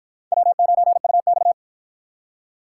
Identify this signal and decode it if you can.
Morse code, heard as a keyed tone, about 53 wpm, 700 Hz. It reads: W9UX